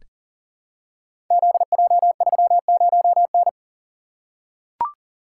Morse code, heard as a keyed tone, about 39 wpm, 690 Hz. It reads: ZJ30N
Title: Code Groups - 5 Letter:Numbers - 40wpm